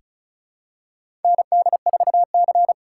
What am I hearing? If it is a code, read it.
Morse code, heard as a keyed tone, about 34 wpm, 700 Hz. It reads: ND4C
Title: USA Call Signs - 4 Characters - 2x Character Spacing - 35wpm